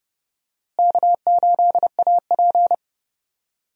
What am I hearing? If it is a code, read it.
Morse code, heard as a keyed tone, about 30 wpm, 700 Hz. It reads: K8AP
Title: USA Call Signs - 4 Characters - Repeated 2x - 30wpm